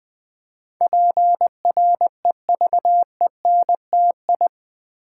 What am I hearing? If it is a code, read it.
Morse code, heard as a keyed tone, about 20 wpm, 700 Hz. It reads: PREVENTI
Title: Build A Word - 4 Letters At A Time - 10-Letter Words - 20wpm